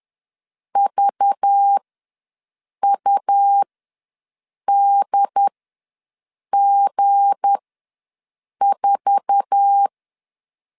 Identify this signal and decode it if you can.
Morse code, heard as a keyed tone, about 10 wpm, spaced as for 6 wpm, 800 Hz. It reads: VUDG4
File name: LEZIONI / 10 SIGNS / 10D COMA ws 10wpm 4314